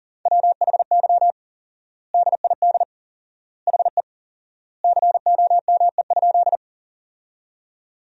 Morse code, heard as a keyed tone, about 40 wpm, 700 Hz. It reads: WHY DID HE COME?
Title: 4x Word-Spacing - Sentences from Top 200 Words - 40wpm